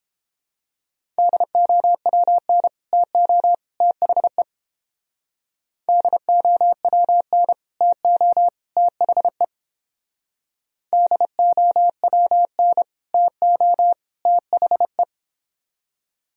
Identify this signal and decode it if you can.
Morse code, heard as a keyed tone, about 29 wpm, 700 Hz. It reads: DOWN TO THE DOWN TO THE DOWN TO THE